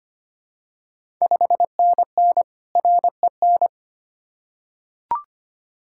Morse code, heard as a keyed tone, about 25 wpm, 700 Hz. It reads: 5NN REN
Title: Ontario QSO Party Exchange 25wpm